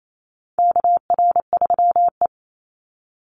Morse code, heard as a keyed tone, about 28 wpm, 700 Hz. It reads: KR3E
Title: Call Signs Repeated 2x - 28wpm